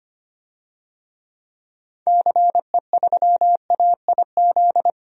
Morse code, heard as a keyed tone, about 25 wpm, 700 Hz. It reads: CE3AIZ
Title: DXCC-100 Call Signs - 6 Characters - 25wpm